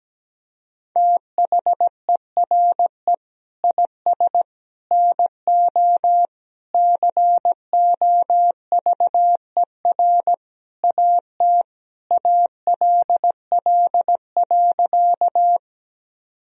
Morse code, heard as a keyed tone, about 17 wpm, 700 Hz. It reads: THERE IS NO COVER AT ALL.
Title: Sentences from Top 1000 Words - Review Part 1 - 17wpm